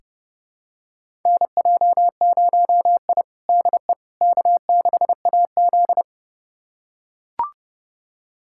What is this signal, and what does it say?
Morse code, heard as a keyed tone, about 30 wpm, 700 Hz. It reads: NJ0I DE K6AZ
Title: USA Callsign DE Callsign - 4 Characters - 30wpm